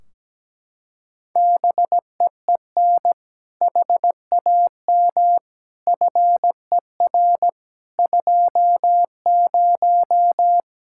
Morse code, heard as a keyed tone, about 17 wpm, 700 Hz. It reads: BEEN HAM FER 20
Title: Practice QSO - Exchange 3 - More Info - 17wpm